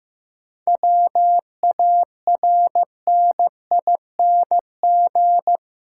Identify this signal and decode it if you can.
Morse code, heard as a keyed tone, about 15 wpm, 700 Hz. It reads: WARNING
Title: Seven Letter Words - 15wpm